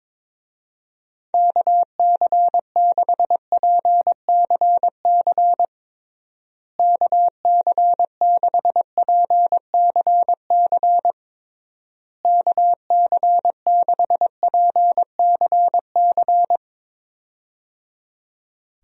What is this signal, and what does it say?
Morse code, heard as a keyed tone, about 22 wpm, 700 Hz. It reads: KC6PCC KC6PCC KC6PCC
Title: USA Call Signs - 6 Characters - Repeated 3x - 22wpm